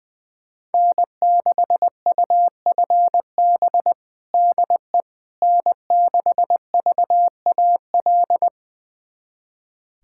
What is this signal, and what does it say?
Morse code, heard as a keyed tone, about 20 wpm, 700 Hz. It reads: N6UFB DE N6VAL
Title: USA Callsign DE Callsign - 5 Characters - 20wpm